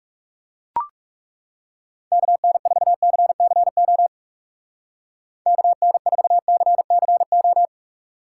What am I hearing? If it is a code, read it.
Morse code, heard as a keyed tone, about 37 wpm, 700 Hz. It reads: KN4CCO KN4CCO
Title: Speed Racing - USA Call Signs 30wpm